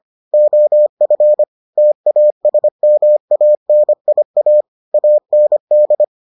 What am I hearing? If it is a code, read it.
Morse code, heard as a keyed tone, about 25 wpm, 600 Hz. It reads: OF TASMANIA AND